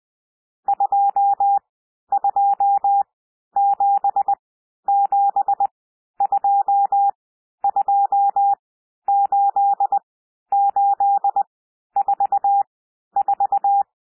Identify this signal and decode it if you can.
Morse code, heard as a keyed tone, about 20 wpm, spaced as for 11 wpm, 800 Hz. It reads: 2277228844
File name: SOUND NUMBERS 20 wpm